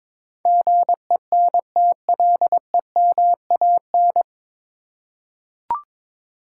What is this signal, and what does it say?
Morse code, heard as a keyed tone, about 22 wpm, 700 Hz. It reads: GENTLEMAN
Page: Morse Code Ninja practice 22wpm